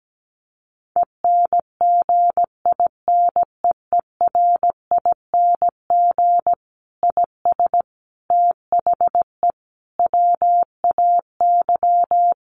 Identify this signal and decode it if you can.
Morse code, heard as a keyed tone, about 17 wpm, 700 Hz. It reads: ENGINEERING IS THE WAY
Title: Sentences from Top 1500 Words 17wpm